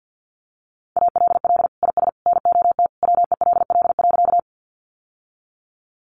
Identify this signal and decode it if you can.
Morse code, heard as a keyed tone, about 50 wpm, 700 Hz. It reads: ALL IS NOT WELL.
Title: Sentences from Top 600 Words 50wpm